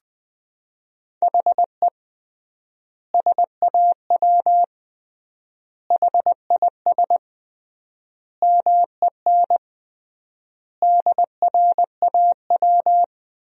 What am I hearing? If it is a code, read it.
Morse code, heard as a keyed tone, about 20 wpm, 700 Hz. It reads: HE SAW HIS MEN DRAW
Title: 3x Word-Spacing - Sentences from Top 300 Words - 20wpm